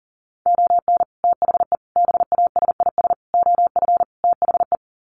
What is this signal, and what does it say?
Morse code, heard as a keyed tone, about 40 wpm, 700 Hz. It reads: ON THE BASIS OF THE
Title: Sets of 5 Words 40wpm